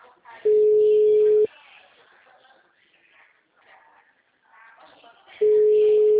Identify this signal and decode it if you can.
Ringback tone (Europe, 425 Hz), with speech.